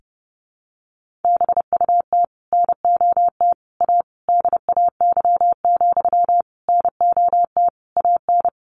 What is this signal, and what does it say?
Morse code, heard as a keyed tone, about 30 wpm, 700 Hz. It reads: BUT NOT A DAY, NOT AN
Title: Sentences from Top 1400 Words 30wpm